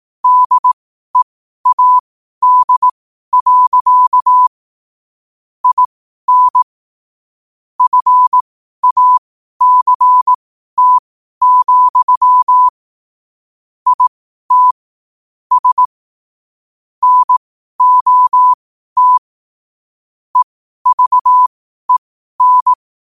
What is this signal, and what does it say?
Morse code, heard as a keyed tone, about 18 wpm, spaced as for 12 wpm, 1000 Hz. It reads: DEAD. IN FACT, IT S NOT EVEN
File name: qotdfiles / 12wpm